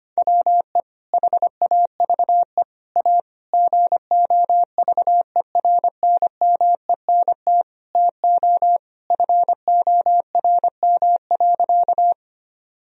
Morse code, heard as a keyed tone, about 25 wpm, 700 Hz. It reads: WE HAVE A GOVERNMENT TO FORM.
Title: Sentences from Top 1000 Words - Review Part 1 - 25wpm